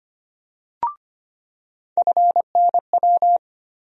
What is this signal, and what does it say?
Morse code, heard as a keyed tone, about 25 wpm, 700 Hz. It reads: FNW
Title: USA Call Sign Postfixes - 3 Letters 25wpm